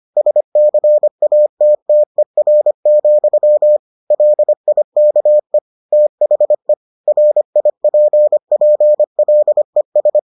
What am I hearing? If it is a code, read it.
Morse code, heard as a keyed tone, about 25 wpm, 600 Hz. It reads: SCATTER, LIKE THE RIPPLES